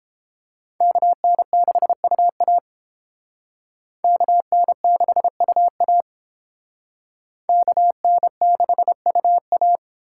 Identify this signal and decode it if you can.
Morse code, heard as a keyed tone, about 30 wpm, 700 Hz. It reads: KN6UA KN6UA KN6UA